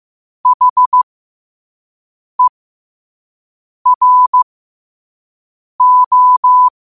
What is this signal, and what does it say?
Morse code, heard as a keyed tone, about 15 wpm, spaced as for 5 wpm, 1000 Hz. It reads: HERO
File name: qotdfiles / 5wpm